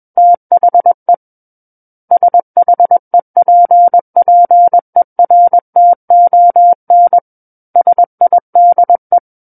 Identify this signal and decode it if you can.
Morse code, heard as a keyed tone, about 21 wpm, 700 Hz. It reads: THE SHEPPERTON SIDE
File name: Book One / 12-Chapter 12 - 21 WPM